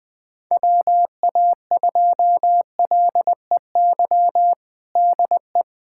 Morse code, heard as a keyed tone, about 21 wpm, 700 Hz. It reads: WA2LEY DE
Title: USA Callsign DE Callsign 20wpm